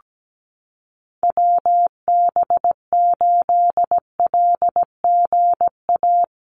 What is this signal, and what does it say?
Morse code, heard as a keyed tone, about 17 wpm, 700 Hz. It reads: WB8LGA
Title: USA Call Signs Repeated 2x 17wpm